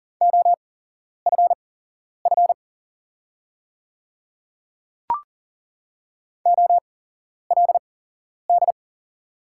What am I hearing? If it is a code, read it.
Morse code, heard as a keyed tone, about 40 wpm, spaced as for 11 wpm, 700 Hz. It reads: OFF OLD
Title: 8x Character Spacing - Top 200 Words - 40wpm